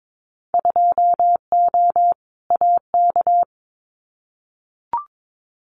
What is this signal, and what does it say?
Morse code, heard as a keyed tone, about 22 wpm, 700 Hz. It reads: 2O AK
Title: Winter Field Day Exchanges Encore 22wpm